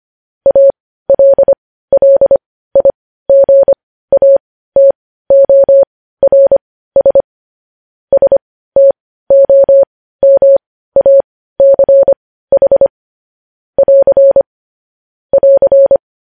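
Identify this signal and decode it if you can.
Morse code, heard as a keyed tone, about 25 wpm, spaced as for 15 wpm, 550 Hz. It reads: ALLIGATORS STOMACH <AR> <AR>
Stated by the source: Morse code news headlines broadcast.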